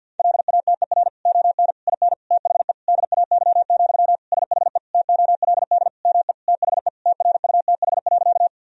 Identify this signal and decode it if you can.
Morse code, heard as a keyed tone, about 50 wpm, 700 Hz. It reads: LATER ON IN THE DAY, SHE TOLD ME THE TRUTH.
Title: Sentences from Top 1000 Words - Review Part 4 - 50wpm